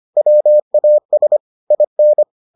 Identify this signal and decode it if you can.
Morse code, heard as a keyed tone, about 25 wpm, 600 Hz. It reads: WAS IN